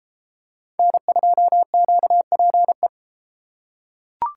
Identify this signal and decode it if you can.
Morse code, heard as a keyed tone, about 32 wpm, 700 Hz. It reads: N2QPE